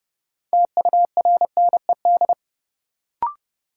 Morse code, heard as a keyed tone, about 31 wpm, 700 Hz. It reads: TURNED
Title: Sets of 4 Words - New Word Drill - 30wpm